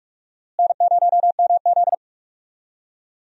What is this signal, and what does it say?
Morse code, heard as a keyed tone, about 46 wpm, 700 Hz. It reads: N0MZ
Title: USA Call Signs - 4 Characters - Repeated 2x - 45wpm